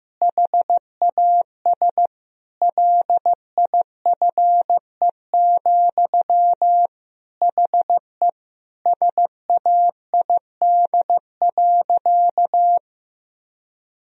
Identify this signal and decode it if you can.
Morse code, heard as a keyed tone, about 15 wpm, 700 Hz. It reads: HAS LIFE, HE SAID.
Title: Sentences from Top 1000 Words - Review Part 1 - 15wpm